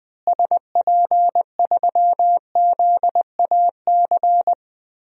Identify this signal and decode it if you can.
Morse code, heard as a keyed tone, about 21 wpm, 700 Hz. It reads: SP3ZAC
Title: DXCC-100 Call Signs - 6 Characters - Repeated 3x - 20wpm